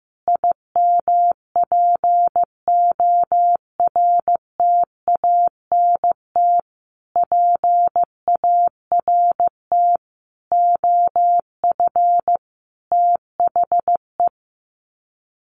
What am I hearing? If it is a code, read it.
Morse code, heard as a keyed tone, about 15 wpm, 700 Hz. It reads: IMPORTANT PART OF THE
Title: Sets of 5 Words Encore 15wpm